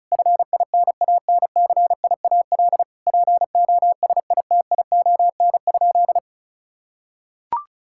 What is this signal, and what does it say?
Morse code, heard as a keyed tone, about 34 wpm, 700 Hz. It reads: FINANCIAL POSITION?
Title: Sentences from Top 1000 Words - Review Part 5 - 35wpm